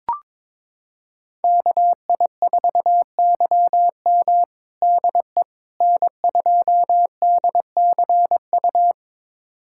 Morse code, heard as a keyed tone, about 22 wpm, 700 Hz. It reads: KI4YM DE N2DCU
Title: USA Callsign DE Callsign - 5 Characters - 22wpm